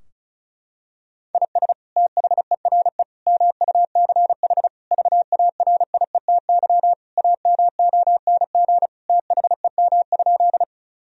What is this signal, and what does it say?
Morse code, heard as a keyed tone, about 34 wpm, 700 Hz. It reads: IS THERE MUCH VARIETY AMONG THEM?
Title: Sentences from Top 1000 Words - Review Part 4 - 35wpm